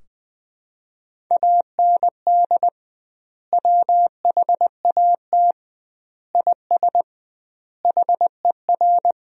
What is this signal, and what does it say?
Morse code, heard as a keyed tone, about 20 wpm, 700 Hz. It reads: AND WHAT IS HER
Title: 2x Word-Spacing - Sentences from Top 200 Words - 20wpm